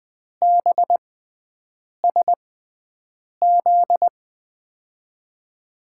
Morse code, heard as a keyed tone, about 20 wpm, spaced as for 7 wpm, 700 Hz. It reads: BSZ
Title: USA Call Signs - 6x Character Spacing - 20wpm